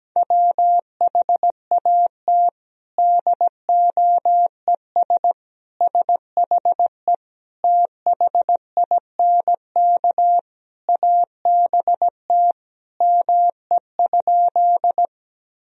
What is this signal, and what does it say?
Morse code, heard as a keyed tone, about 17 wpm, 700 Hz. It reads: WHAT DOES SHE THINK ABT ME?
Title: Sentences from Top 200 Words and Abbreviations 17wpm